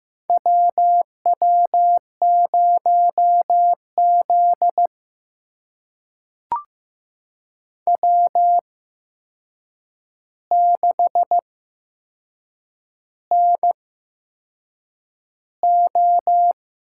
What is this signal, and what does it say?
Morse code, heard as a keyed tone, about 15 wpm, spaced as for 6 wpm, 700 Hz. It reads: WW0Z W6NO
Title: USA Call Signs - 4 Characters - 8x Character Spacing - 15wpm